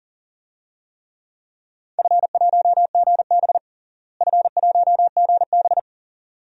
Morse code, heard as a keyed tone, about 40 wpm, 700 Hz. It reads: F1GB F1GB
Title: DXCC-100 Call Signs - 4 Characters - Repeated 2x - 40wpm